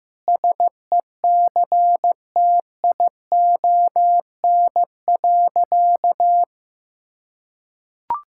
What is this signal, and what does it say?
Morse code, heard as a keyed tone, about 15 wpm, 700 Hz. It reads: SECTION.
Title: Sentences from Top 1000 Words - Review Part 5 - 15wpm